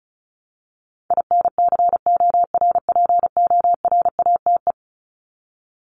Morse code, heard as a keyed tone, about 34 wpm, 700 Hz. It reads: INCORPORATE
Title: Top 1751-2000 Words 35wpm